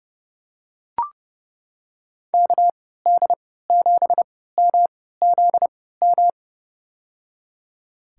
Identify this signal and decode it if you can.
Morse code, heard as a keyed tone, about 32 wpm, spaced as for 17 wpm, 700 Hz. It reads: KD7MZM
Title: USA Call Signs - 3x Character Spacing - 30wpm